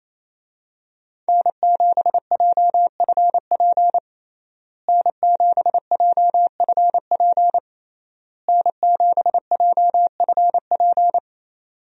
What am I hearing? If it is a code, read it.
Morse code, heard as a keyed tone, about 28 wpm, 700 Hz. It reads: N7JFP N7JFP N7JFP
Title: USA Call Signs - 5 Characters - Repeated 3x - 28wpm